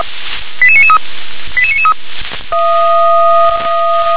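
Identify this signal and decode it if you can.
Five-tone selective call (ZVEI), heard as quick runs of five tones, tones of about 70 ms. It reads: 88993 88993
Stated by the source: ZVEI five-tone call received on an SDR.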